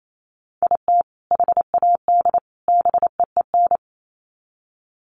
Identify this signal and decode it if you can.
Morse code, heard as a keyed tone, about 28 wpm, 700 Hz. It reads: IT HAD BEEN
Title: Sets of 3 Words 28wpm